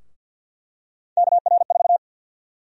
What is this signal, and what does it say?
Morse code, heard as a keyed tone, about 46 wpm, 700 Hz. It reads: KR4
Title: USA Call Sign Prefixes - Repeated 2x - 50wpm